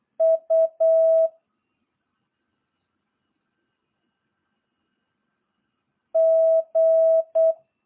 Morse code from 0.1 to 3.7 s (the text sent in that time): U